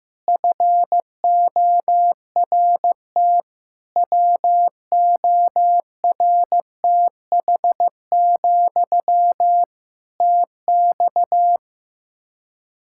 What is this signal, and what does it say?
Morse code, heard as a keyed tone, about 15 wpm, 700 Hz. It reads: FORT WORTH, TX
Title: Practice QSO - Exchange 1 - Beginning - 15wpm